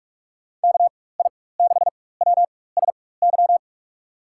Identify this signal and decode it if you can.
Morse code, heard as a keyed tone, about 43 wpm, spaced as for 21 wpm, 700 Hz. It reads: KI6WSY
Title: USA Call Signs - 6 Characters - 4x Character Spacing - 45wpm